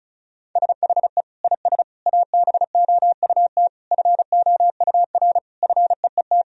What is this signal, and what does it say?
Morse code, heard as a keyed tone, about 34 wpm, 700 Hz. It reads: SHE IS ABOUT FOUR FEET
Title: Sentences from Top 1000 Words - Review Part 1 - 35wpm